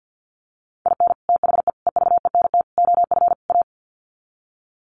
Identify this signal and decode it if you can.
Morse code, heard as a keyed tone, about 50 wpm, 700 Hz. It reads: IN THE EVENT OF A
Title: Sets of 5 Words Encore 50wpm